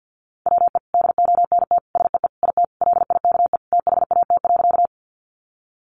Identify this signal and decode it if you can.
Morse code, heard as a keyed tone, about 50 wpm, 700 Hz. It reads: WE DONT SEE IT LIKE THAT.
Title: Sentences from Top 300 Words 50wpm